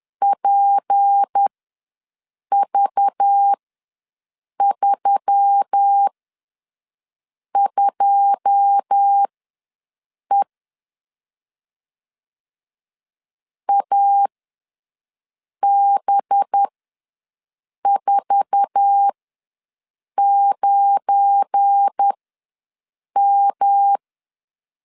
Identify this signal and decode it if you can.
Morse code, heard as a keyed tone, about 10 wpm, spaced as for 5 wpm, 800 Hz. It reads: PV32E AB49M
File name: LEZIONI / 9 2 ws 10wpm 4130